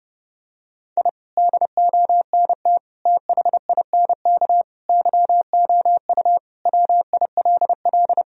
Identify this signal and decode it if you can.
Morse code, heard as a keyed tone, about 30 wpm, 700 Hz. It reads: I DONT THINK YOU WILL
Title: Sentences from Top 1000 Words - Review Part 4 - 30wpm